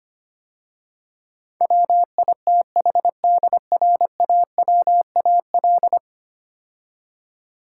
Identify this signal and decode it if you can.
Morse code, heard as a keyed tone, about 25 wpm, 700 Hz. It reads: WITHDRAWAL
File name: 10 Letter Words - First 3 Letters Given - 25wpm